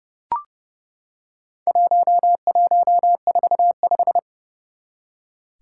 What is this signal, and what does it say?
Morse code, heard as a keyed tone, about 30 wpm, 700 Hz. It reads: 1145
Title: Four Challenging Letter-Number Combinations 30wpm